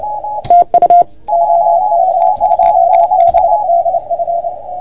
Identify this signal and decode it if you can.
Morse code, heard as a keyed tone, about 32 wpm, 680 Hz, with radio noise. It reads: TU OK2BDF
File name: Archiv / CUC381 / OK1FGD CUC 381